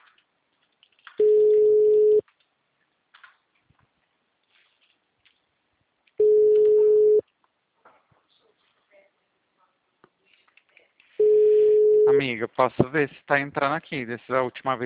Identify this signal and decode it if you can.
Ringback tone (Europe, 425 Hz), with speech.